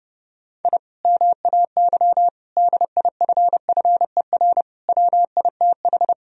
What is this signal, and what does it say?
Morse code, heard as a keyed tone, about 30 wpm, 700 Hz. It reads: I MAY DIFFER WITH
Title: Sentences from Top 1000 Words - Review Part 1 - 30wpm